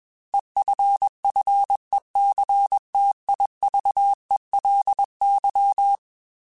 Morse code, heard as a keyed tone, about 21 wpm, 800 Hz. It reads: EFFECTIVELY
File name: Technical Subjects / Technical Subjects-mp3s / XC @20